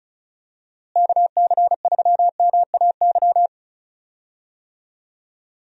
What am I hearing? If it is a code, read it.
Morse code, heard as a keyed tone, about 36 wpm, 700 Hz. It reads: KC3MAY
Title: USA Call Signs - 6 Characters - Repeated 3x - 35wpm